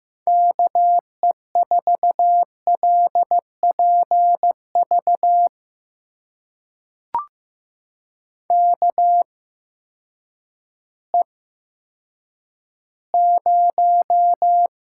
Morse code, heard as a keyed tone, about 15 wpm, spaced as for 9 wpm, 700 Hz. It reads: KE4LPV KE0
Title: USA Call Signs - 6 Characters - 8x Character Spacing - 15wpm